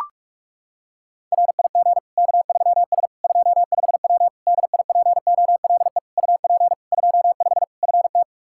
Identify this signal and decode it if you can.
Morse code, heard as a keyed tone, about 43 wpm, 700 Hz. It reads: RIG K3S 25W DIPOLE UP 25 FT
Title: K1USN SST QSO Practice 45wpm